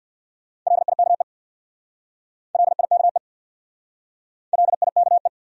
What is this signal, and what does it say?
Morse code, heard as a keyed tone, about 67 wpm, 700 Hz. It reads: LIKE LIKE LIKE